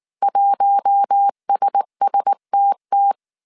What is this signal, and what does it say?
Morse code, heard as a keyed tone, about 19 wpm, 800 Hz. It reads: 1SSTT
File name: CW / DET / 1 6 1500C S A 18wpm 2255